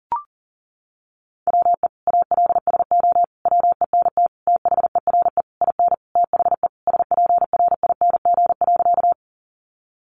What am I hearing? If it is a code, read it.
Morse code, heard as a keyed tone, about 40 wpm, 700 Hz. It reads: WE ALSO WENT THERE IN THE SPRING.
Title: Sentences from Top 1200 Words 40wpm